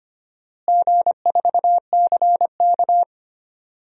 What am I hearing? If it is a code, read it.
Morse code, heard as a keyed tone, about 25 wpm, 700 Hz. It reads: G4CK
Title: DXCC-100 Call Signs - 4 Characters - Repeated 3x - 25wpm